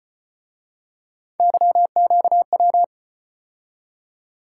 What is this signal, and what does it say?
Morse code, heard as a keyed tone, about 34 wpm, 700 Hz. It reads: YQW